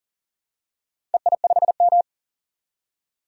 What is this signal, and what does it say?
Morse code, heard as a keyed tone, about 40 wpm, 700 Hz. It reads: EI5M
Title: DXCC-100 Call Signs - 4 Characters - 40wpm